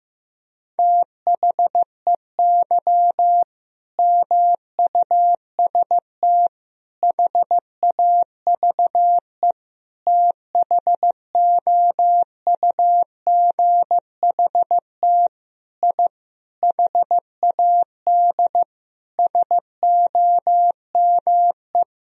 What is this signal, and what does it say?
Morse code, heard as a keyed tone, about 15 wpm, 700 Hz. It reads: THEY MUST HAVE THOUGHT I HAD SOME
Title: Sentences from Top 1000 Words - Review Part 4 - 15wpm